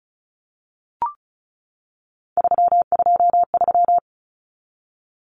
Morse code, heard as a keyed tone, about 34 wpm, 700 Hz. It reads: 323
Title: Common Cut Numbers - 3-Digit 35wpm